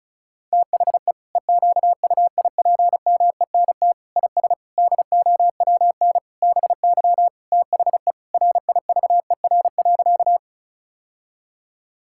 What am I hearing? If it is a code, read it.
Morse code, heard as a keyed tone, about 36 wpm, 700 Hz. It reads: THE EQUIPMENT IS DOWN BY THE RIVER.
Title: Sentences from Top 1000 Words - Review Part 5 - 35wpm